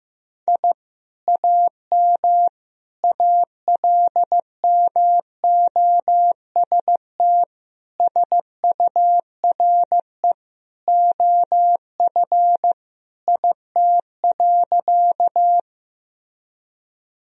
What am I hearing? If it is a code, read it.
Morse code, heard as a keyed tone, about 15 wpm, 700 Hz. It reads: I AM ALMOST SURE OF IT.
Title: Sentences from Top 1000 Words - Review Part 4 - 15wpm